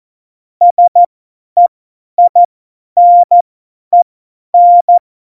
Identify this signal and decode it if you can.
Morse code, heard as a keyed tone, about 14 wpm, spaced as for 10 wpm, 700 Hz. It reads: SEINEN
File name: Morsekurs / Witze / Witz 08. September 2020, 50 F70